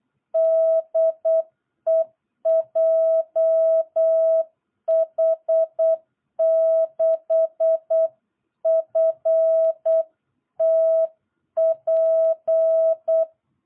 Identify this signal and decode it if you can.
Morse code, heard as a keyed tone, about 8 wpm, 660 Hz. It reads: DE JH6FTP
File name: jh8cbh / 2022cw